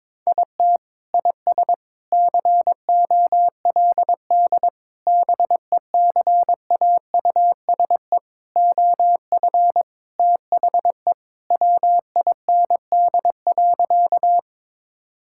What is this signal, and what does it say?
Morse code, heard as a keyed tone, about 21 wpm, 700 Hz. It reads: IT IS COLD BECAUSE OF THE WIND.
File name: practice / 22wpm / Sentences with 3-Word Phrases - 22wpm